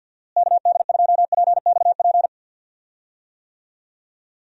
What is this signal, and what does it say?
Morse code, heard as a keyed tone, about 52 wpm, 700 Hz. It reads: KD2PXP
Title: USA Call Signs - 6 Characters - Repeated 3x - 50wpm